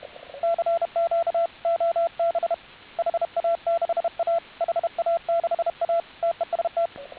Morse code, heard as a keyed tone, about 32 wpm, 680 Hz, with radio noise. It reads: CQ OB HA6A HA6A TEST